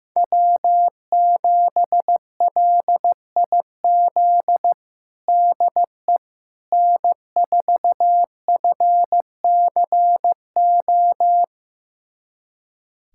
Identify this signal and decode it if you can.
Morse code, heard as a keyed tone, about 15 wpm, 700 Hz. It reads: W7LIZ DE N4FCO
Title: USA Callsign DE Callsign - 5 Characters - 15wpm